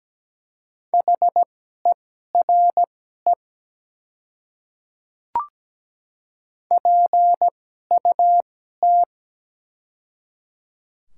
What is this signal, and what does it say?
Morse code, heard as a keyed tone, about 17 wpm, spaced as for 12 wpm, 700 Hz. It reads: HERE PUT
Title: 2x Character Spacing - Top 200 Words - 17wpm